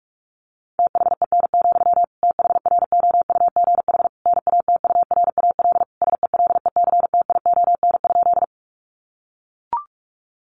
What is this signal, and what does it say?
Morse code, heard as a keyed tone, about 44 wpm, 700 Hz. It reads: THEN, THROUGH NATURAL SELECTION?
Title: Sentences from Top 1400 Words 45wpm